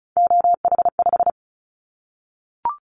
Morse code, heard as a keyed tone, about 36 wpm, 700 Hz. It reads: OH5
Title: Callsign Prefixes for Top 50 DXCC - 35wpm